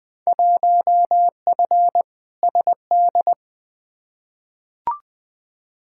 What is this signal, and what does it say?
Morse code, heard as a keyed tone, about 20 wpm, 700 Hz. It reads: 1F SD
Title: Field Day Exchanges - 20wpm